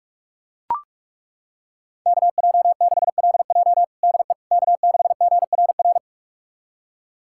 Morse code, heard as a keyed tone, about 41 wpm, 700 Hz. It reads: KJ6LJ DE K6GRR
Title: USA Callsign DE Callsign - 5 Characters - 45wpm